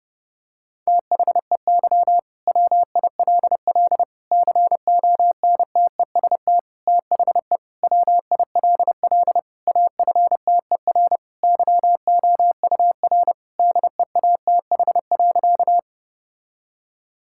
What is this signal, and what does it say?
Morse code, heard as a keyed tone, about 30 wpm, 700 Hz. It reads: THEY WILL CONTEST THE WILL AFTER YOUR DEATH.
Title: Sentences from Top 1600 Words 30wpm